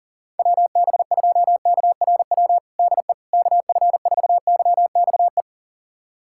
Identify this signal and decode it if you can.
Morse code, heard as a keyed tone, about 40 wpm, 700 Hz. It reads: WB2KRW DE KF4YXE
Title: USA Callsign DE Callsign - 6 Characters - 40wpm